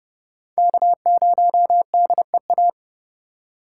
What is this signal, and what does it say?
Morse code, heard as a keyed tone, about 32 wpm, 700 Hz. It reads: K0DEA
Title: USA Call Signs - 5 Characters - 3x Character Spacing - 30wpm